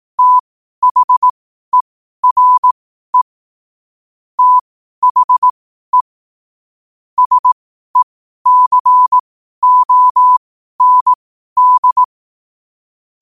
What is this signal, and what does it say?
Morse code, heard as a keyed tone, about 18 wpm, spaced as for 12 wpm, 1000 Hz. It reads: THERE THE SECOND